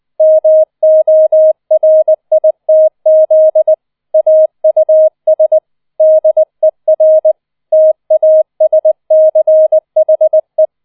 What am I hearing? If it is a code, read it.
Morse code, heard as a keyed tone, about 20 wpm, 630 Hz. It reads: MORITZ AUS DER TASCHE